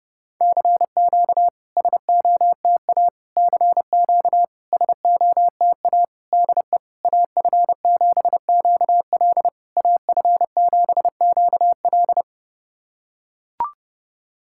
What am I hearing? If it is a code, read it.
Morse code, heard as a keyed tone, about 30 wpm, 700 Hz. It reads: CQ SOTA CQ SOTA DE AF7QL AF7QL
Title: SOTA QSO Practice 30wpm